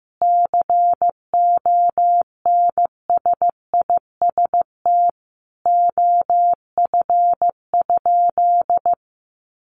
Morse code, heard as a keyed tone, about 15 wpm, 700 Hz. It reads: CONSIST OF?
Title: Sentences from Top 1400 Words 15wpm